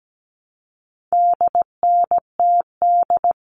Morse code, heard as a keyed tone, about 17 wpm, 700 Hz. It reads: DNTD
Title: All Cut Numbers - 4-Digit 17wpm